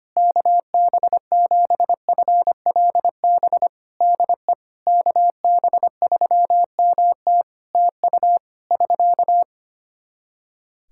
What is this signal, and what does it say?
Morse code, heard as a keyed tone, about 25 wpm, 700 Hz. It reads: KB7FLB DE KB3MT TU <SK>
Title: Practice QSO - Exchange 4 - Ending - 25wpm